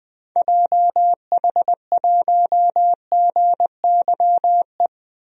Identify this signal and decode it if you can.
Morse code, heard as a keyed tone, about 20 wpm, 700 Hz. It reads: JH1GYE
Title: DXCC-100 Call Signs - 6 Characters - 20wpm